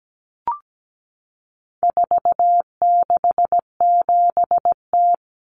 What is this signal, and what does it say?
Morse code, heard as a keyed tone, about 17 wpm, 700 Hz. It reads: 467T
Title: Common Cut Numbers - 4-Digit 17wpm